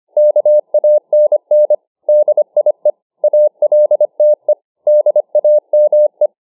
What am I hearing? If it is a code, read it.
Morse code, heard as a keyed tone, about 25 wpm, 600 Hz. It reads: KANN DIE ALTE DAME